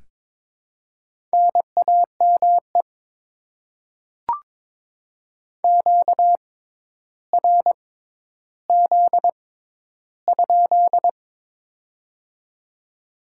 Morse code, heard as a keyed tone, about 22 wpm, spaced as for 11 wpm, 700 Hz. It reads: NAME QRZ?
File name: Top 90 QSO Elements - 6x Character Spacing - 22wpm